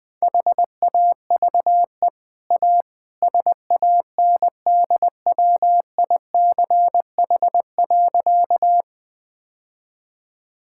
Morse code, heard as a keyed tone, about 21 wpm, 700 Hz. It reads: HAVE A SANDWICH.
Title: Easy Sentences with Hints - 20wpm